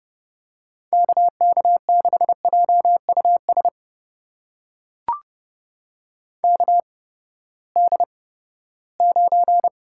Morse code, heard as a keyed tone, about 30 wpm, spaced as for 17 wpm, 700 Hz. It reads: KK6JUS KD9
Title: USA Call Signs - 6 Characters - 8x Character Spacing - 30wpm